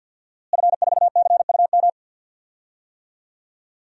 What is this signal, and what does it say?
Morse code, heard as a keyed tone, about 48 wpm, 700 Hz. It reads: F4CUM
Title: DXCC-100 Call Signs - 5 Characters - Repeated 2x - 50wpm